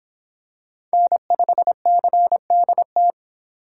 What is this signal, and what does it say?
Morse code, heard as a keyed tone, about 25 wpm, 700 Hz. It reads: N5CDT